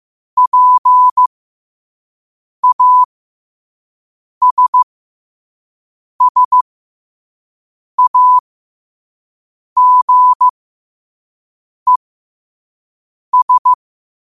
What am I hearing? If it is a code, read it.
Morse code, heard as a keyed tone, about 15 wpm, spaced as for 5 wpm, 1000 Hz. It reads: PASSAGES
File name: qotdfiles / 5wpm